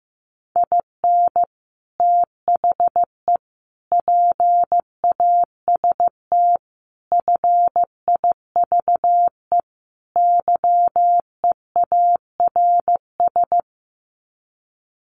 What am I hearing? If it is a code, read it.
Morse code, heard as a keyed tone, about 15 wpm, 700 Hz. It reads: IN THE PAST FIVE YEARS
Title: Sets of 5 Words Encore 15wpm